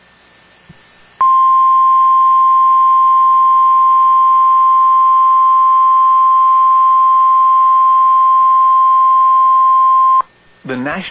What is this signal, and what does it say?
Emergency Alert System (SAME) tones (the 1050 Hz attention tone), with speech. It is heard from 1.1 to 10.2 s.